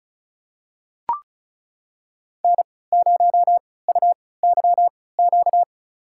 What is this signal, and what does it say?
Morse code, heard as a keyed tone, about 36 wpm, spaced as for 20 wpm, 700 Hz. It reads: N0UYQ
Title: USA Call Signs - 5 Characters - 3x Character Spacing - 35wpm